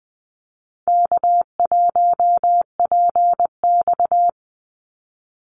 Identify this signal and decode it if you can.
Morse code, heard as a keyed tone, about 19 wpm, 700 Hz. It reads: K1PX
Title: Easy Call Signs Repeated 3x - 20wpm